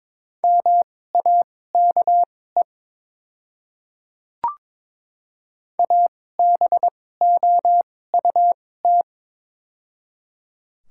Morse code, heard as a keyed tone, about 22 wpm, spaced as for 16 wpm, 700 Hz. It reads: MAKE ABOUT
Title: 2x Character Spacing - Top 200 Words - 22wpm